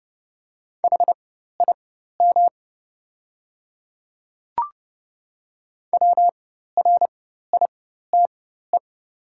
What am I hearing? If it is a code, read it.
Morse code, heard as a keyed tone, about 30 wpm, spaced as for 14 wpm, 690 Hz. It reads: HIM WRITE
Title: 4x Character Spacing - Top 100 Words - 30wpm